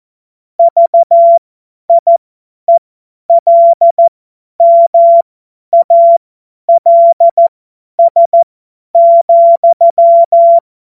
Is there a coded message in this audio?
Yes: Morse code from 0.6 to 10.6 s, heard as a keyed tone, about 14 wpm, spaced as for 10 wpm, 680 Hz. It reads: VIELMALS,